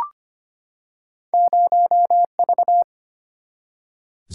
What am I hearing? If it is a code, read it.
Morse code, heard as a keyed tone, about 25 wpm, 700 Hz. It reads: 0V
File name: Two Letter-Number Combinations - 25wpm